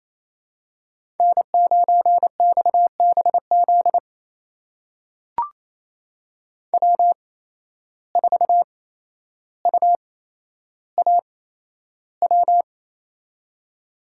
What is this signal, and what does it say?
Morse code, heard as a keyed tone, about 28 wpm, spaced as for 12 wpm, 700 Hz. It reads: N9XBZ W4UAW
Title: USA Call Signs - 5 Characters - 8x Character Spacing - 28wpm